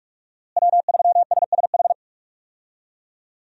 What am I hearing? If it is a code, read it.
Morse code, heard as a keyed tone, about 41 wpm, 700 Hz. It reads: W3SSH
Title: USA Call Signs - 5 Characters - Repeated 2x - 45wpm